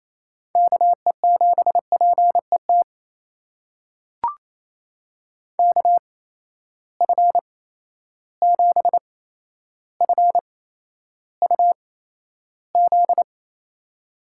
Morse code, heard as a keyed tone, about 28 wpm, spaced as for 12 wpm, 700 Hz. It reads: KE7PET KF7FUZ
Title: USA Call Signs - 6 Characters - 8x Character Spacing - 28wpm